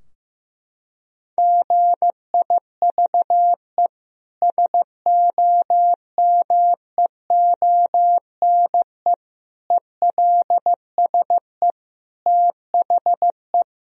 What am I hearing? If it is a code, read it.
Morse code, heard as a keyed tone, about 14 wpm, 700 Hz. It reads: GIVE SOMEONE ELSE THE
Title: Sentences from Top 1000 Words - Review Part 4 - 15wpm